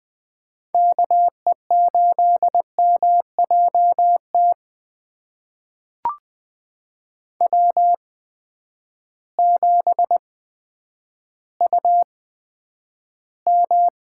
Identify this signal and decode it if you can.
Morse code, heard as a keyed tone, about 20 wpm, spaced as for 10 wpm, 700 Hz. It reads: KE8MJT W7UM
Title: USA Call Signs - 8x Character Spacing - 20wpm